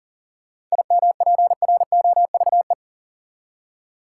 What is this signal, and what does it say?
Morse code, heard as a keyed tone, about 40 wpm, 700 Hz. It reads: IMPROVE
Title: Seven Letter Words - 40wpm